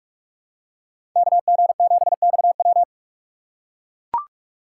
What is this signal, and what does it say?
Morse code, heard as a keyed tone, about 44 wpm, 700 Hz. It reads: KG7XW
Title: USA Call Signs - 5 Characters - 6x Character Spacing - 45wpm